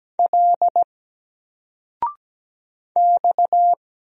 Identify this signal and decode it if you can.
Morse code, heard as a keyed tone, about 17 wpm, 700 Hz. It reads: L X
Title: Single Letter-Number Alt 17wpm